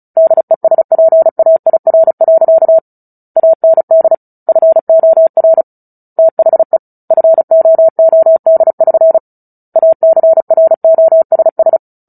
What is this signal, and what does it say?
Morse code, heard as a keyed tone, about 34 wpm, 650 Hz. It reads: DESPAIR. AND FOR THE FOOD<SN> ACROSS
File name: Book Two / 09-Chapter 9 - 35 WPM